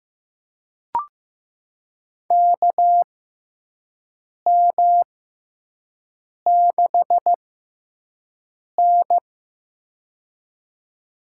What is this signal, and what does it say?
Morse code, heard as a keyed tone, about 15 wpm, spaced as for 5 wpm, 700 Hz. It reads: KM6N
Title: USA Call Signs - 6x Character Spacing - 15wpm